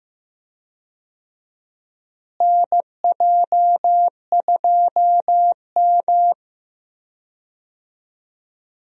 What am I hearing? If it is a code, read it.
Morse code, heard as a keyed tone, about 15 wpm, 700 Hz. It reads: NJ2M
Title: USA Call Signs - 4 Characters - Repeated 2x - 15wpm